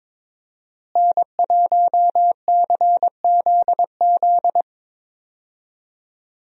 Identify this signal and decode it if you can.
Morse code, heard as a keyed tone, about 22 wpm, 700 Hz. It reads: N1CZZ
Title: USA Call Signs - 5 Characters - Repeated 3x - 22wpm